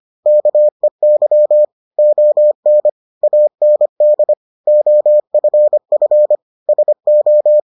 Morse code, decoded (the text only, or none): KEY ON AND OFF SO